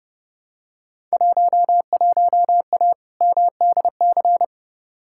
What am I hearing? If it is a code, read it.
Morse code, heard as a keyed tone, about 30 wpm, 700 Hz. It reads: 11A MDC
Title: Field Day Exchanges Encore - 30wpm